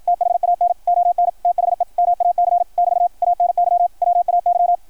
Morse code, heard as a keyed tone, about 53 wpm, 690 Hz. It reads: THAN ON THE MAC <BT> ANY WAY
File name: archived / CRSnet